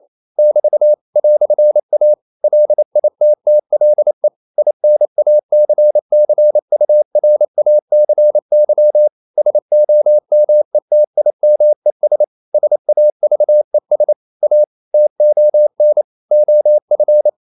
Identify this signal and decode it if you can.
Morse code, heard as a keyed tone, about 28 wpm, 600 Hz. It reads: <BT> "A LITTLE INACCURACY SOMETIMES SAVES A TON OF
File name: CW / Practice / EnglishQuotations0228